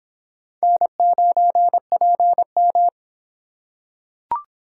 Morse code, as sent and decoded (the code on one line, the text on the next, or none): -. ----. .--. --
N9PM